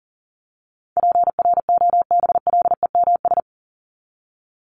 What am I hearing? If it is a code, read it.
Morse code, heard as a keyed tone, about 40 wpm, 700 Hz. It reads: PROBLEMS
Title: ES:S Suffix - Top 100 Words with Base Words 40wpm